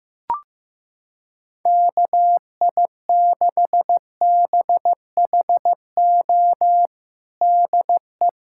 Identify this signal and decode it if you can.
Morse code, heard as a keyed tone, about 15 wpm, 700 Hz. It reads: KI6BHO DE
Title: USA Callsign DE Callsign 15wpm